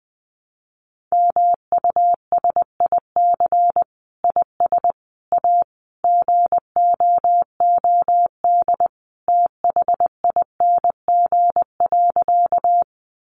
Morse code, heard as a keyed tone, about 20 wpm, 700 Hz. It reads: MUSIC IS A GOOD THING.
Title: Sentences from Top 300 Words 20wpm